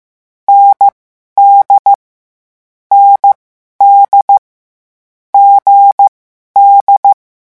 Morse code, heard as a keyed tone, about 15 wpm, spaced as for 11 wpm, 800 Hz. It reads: ND ND GD
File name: morse mp3 / lecon4-vitesse 14